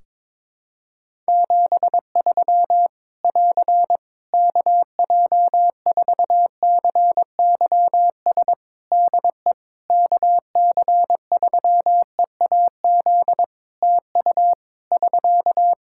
Morse code, heard as a keyed tone, about 22 wpm, 700 Hz. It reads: 73 <AR> KJ4CYS DE KC3EAZ TU <SK>
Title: Practice QSO - Exchange 4 - Ending - 22wpm